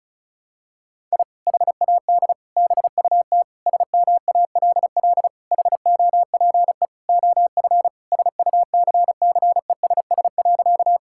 Morse code, heard as a keyed tone, about 36 wpm, 700 Hz. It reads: I HAD BUT SMALL HOPE OF SUCCESS.
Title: Sentences from Top 1000 Words - Review Part 6 - 35wpm